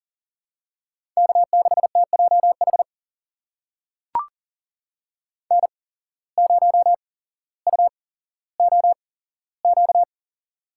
Morse code, heard as a keyed tone, about 40 wpm, spaced as for 17 wpm, 700 Hz. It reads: K6TJH N0UOQ
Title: USA Call Signs - 5 Characters - 8x Character Spacing - 40wpm